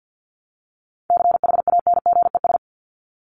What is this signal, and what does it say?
Morse code, heard as a keyed tone, about 48 wpm, 700 Hz. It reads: CHANGES
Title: ES:S Suffix - Top 100 Words with Base Words 50wpm